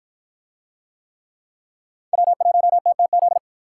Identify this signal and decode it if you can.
Morse code, heard as a keyed tone, about 53 wpm, 700 Hz. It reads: W1TTZ